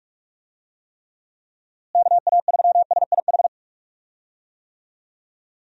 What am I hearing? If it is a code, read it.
Morse code, heard as a keyed tone, about 46 wpm, 700 Hz. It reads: KA3SIH
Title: USA Call Signs - 6 Characters - Repeated 3x - 45wpm